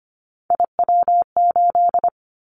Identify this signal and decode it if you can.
Morse code, heard as a keyed tone, about 25 wpm, 700 Hz. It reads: IW8
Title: Callsign Prefixes for Top 50 DXCC - 25wpm